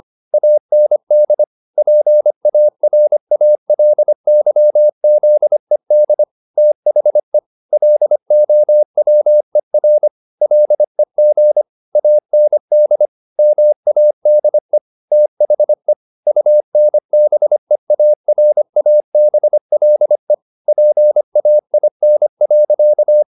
Morse code, heard as a keyed tone, about 25 wpm, 600 Hz. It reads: AND PARALYZED THE LOWER LEG AND MADE THE UNBEARABLE PAIN.